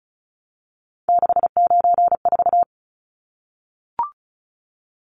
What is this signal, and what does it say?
Morse code, heard as a keyed tone, about 34 wpm, 700 Hz. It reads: 694
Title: Numbers - 3-Digit 35wpm